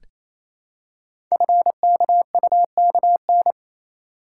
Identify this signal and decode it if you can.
Morse code, heard as a keyed tone, about 29 wpm, 700 Hz. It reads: FKUKN
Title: Code Groups - 5 Letters - 28wpm